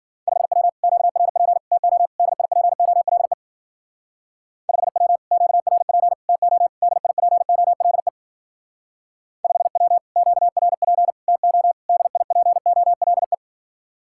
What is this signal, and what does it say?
Morse code, heard as a keyed tone, about 51 wpm, 700 Hz. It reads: 5W QRP TO DIPOLE 5W QRP TO DIPOLE 5W QRP TO DIPOLE